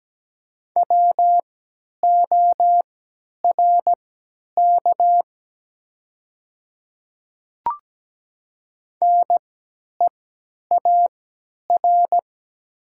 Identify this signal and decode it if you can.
Morse code, heard as a keyed tone, about 17 wpm, spaced as for 10 wpm, 700 Hz. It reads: WORK NEAR
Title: 3x Character Spacing - Top 200 Words - 17wpm